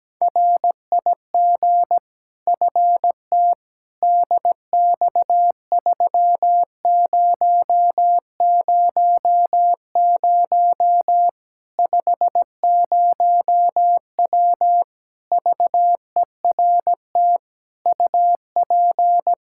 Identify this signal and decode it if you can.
Morse code, heard as a keyed tone, about 17 wpm, 700 Hz. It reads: RIG FT DX3000 50W VERT UP
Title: Practice QSO - Exchange 2 - Info - 17wpm